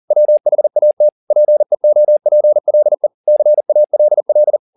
Morse code, heard as a keyed tone, about 40 wpm, 600 Hz. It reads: WHAT PEOPLE CALL